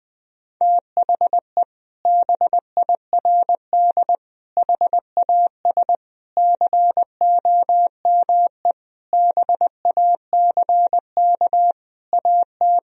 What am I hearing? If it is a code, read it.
Morse code, heard as a keyed tone, about 20 wpm, 700 Hz. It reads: THE BIRD HAS COME BACK AT
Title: Sentences from Top 1000 Words - Review Part 2 - 20wpm